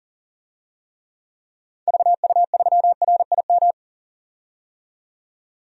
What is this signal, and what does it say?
Morse code, heard as a keyed tone, about 40 wpm, 700 Hz. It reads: VU3RIM
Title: DXCC-100 Call Signs - 6 Characters - 40wpm